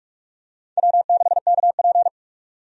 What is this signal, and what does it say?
Morse code, heard as a keyed tone, about 46 wpm, 700 Hz. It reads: W6KP
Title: USA Call Signs - 4 Characters - 45wpm